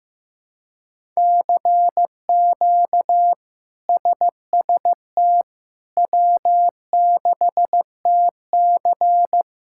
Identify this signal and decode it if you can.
Morse code, heard as a keyed tone, about 15 wpm, 700 Hz. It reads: CQ SST W6TC
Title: K1USN SST QSO Practice - 15wpm